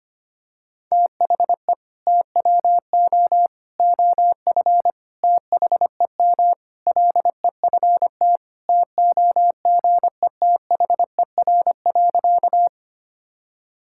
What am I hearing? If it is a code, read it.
Morse code, heard as a keyed tone, about 25 wpm, 700 Hz. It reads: THE TWO OF THEM LEFT TOGETHER.
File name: practice / 25wpm / Sentences with 4-Word Phrases - 25wpm